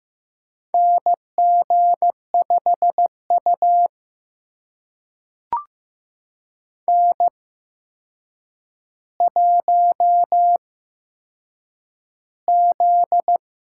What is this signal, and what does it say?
Morse code, heard as a keyed tone, about 15 wpm, spaced as for 7 wpm, 700 Hz. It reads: NG5U N1Z
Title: USA Call Signs - 4 Characters - 8x Character Spacing - 15wpm